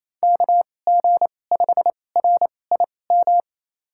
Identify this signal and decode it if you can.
Morse code, heard as a keyed tone, about 28 wpm, spaced as for 20 wpm, 700 Hz. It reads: KG5RIM
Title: USA Call Signs - 6 Characters - 2x Character Spacing - 28wpm